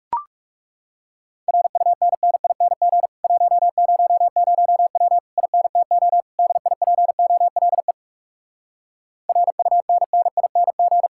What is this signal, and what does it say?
Morse code, heard as a keyed tone, about 40 wpm, 700 Hz. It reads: RUNNING 100W INTO DIPOLE RUNNING